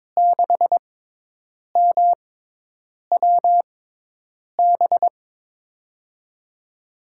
Morse code, heard as a keyed tone, about 22 wpm, spaced as for 8 wpm, 700 Hz. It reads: 6MWB
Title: USA Call Signs - 5 Characters - 6x Character Spacing - 22wpm